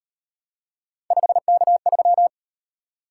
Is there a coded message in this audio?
Yes: Morse code from 1.1 to 2.3 s, heard as a keyed tone, about 38 wpm, 700 Hz. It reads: 5K3